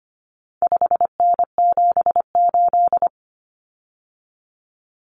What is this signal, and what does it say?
Morse code, heard as a keyed tone, about 25 wpm, 700 Hz. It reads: 5N78
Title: Common Cut Numbers - 4-Digit 25wpm